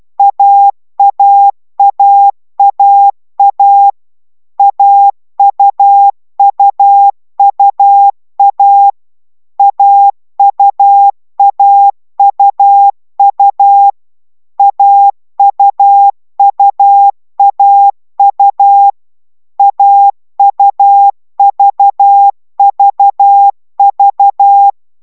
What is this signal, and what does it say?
Morse code, heard as a keyed tone, about 12 wpm, 800 Hz. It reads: AAAAA AUUUA AUAUU AUUAU AUVVV